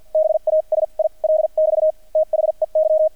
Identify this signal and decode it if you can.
Morse code, heard as a keyed tone, about 50 wpm, spaced as for 40 wpm, 630 Hz. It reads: BASIL <BT> THEY
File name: archived / CRSnet